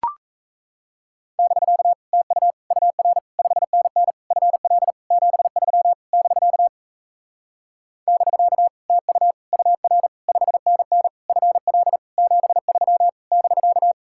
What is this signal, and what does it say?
Morse code, heard as a keyed tone, about 37 wpm, 700 Hz. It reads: <BK> TU UR 5NN FL 73 <BK> <BK> TU UR 5NN FL 73 <BK>